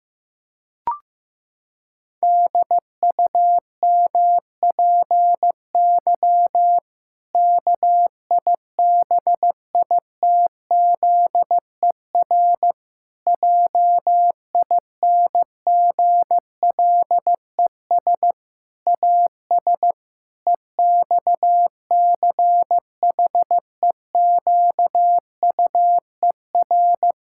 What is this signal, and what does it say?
Morse code, heard as a keyed tone, about 16 wpm, 700 Hz. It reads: DUMPY KIBITZER JINGLES AS EXCHEQUER
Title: Pangram Challenge v2 - 15wpm